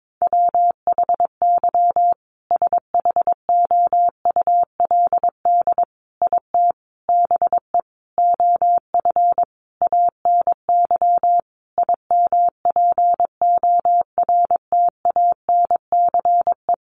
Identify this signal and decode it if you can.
Morse code, heard as a keyed tone, about 22 wpm, 700 Hz. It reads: WHY SHOULD IT BE OF ANY IMPORTANCE
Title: Sentences from Top 1200 Words 22wpm